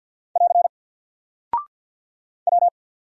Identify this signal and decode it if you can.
Morse code, heard as a keyed tone, about 48 wpm, 700 Hz. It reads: <AR> W
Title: Single Character (Letter:Number:Punctuation:Prosign) Alt 50wpm